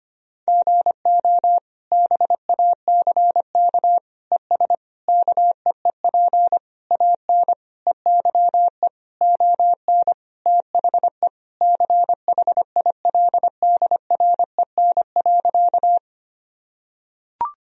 Morse code, heard as a keyed tone, about 25 wpm, 700 Hz. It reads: GO BACK ES KEEP AN EYE ON THE CHILDREN.
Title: Sentences from Top 300 Words and Abbreviations 25wpm